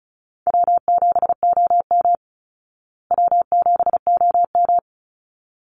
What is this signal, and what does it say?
Morse code, heard as a keyed tone, about 34 wpm, 700 Hz. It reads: W7OM W7OM
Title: Call Signs Repeated 3x - 35wpm